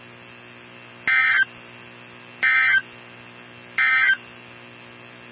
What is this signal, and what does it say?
Emergency Alert System (SAME) tones, heard as harsh buzzing bursts. It is heard from 1.0 to 4.2 s.
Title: NOAA Weather Radio 162.550